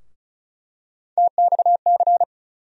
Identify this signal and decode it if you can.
Morse code, heard as a keyed tone, about 36 wpm, 700 Hz. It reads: TXC
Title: USA Call Sign Postfixes - 3 Letters 35wpm